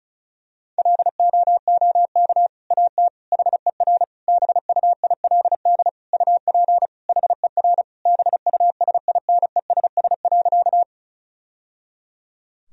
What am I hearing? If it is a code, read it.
Morse code, heard as a keyed tone, about 36 wpm, 700 Hz. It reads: LOOK AT HER BUILD UP HER BUSINESS.
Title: Sentences from Top 1000 Words - Review Part 1 - 35wpm